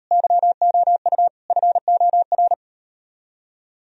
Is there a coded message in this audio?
Yes: Morse code from 0.1 to 2.5 s, heard as a keyed tone, about 38 wpm, 700 Hz. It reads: YOU FOR